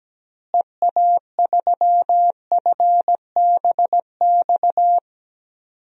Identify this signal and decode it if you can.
Morse code, heard as a keyed tone, about 17 wpm, 700 Hz. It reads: EA3FBX
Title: DXCC-100 Call Signs - 6 Characters - Repeated 2x - 17wpm